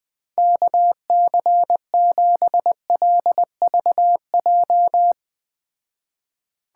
Morse code, heard as a keyed tone, about 21 wpm, 700 Hz. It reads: KC7LVJ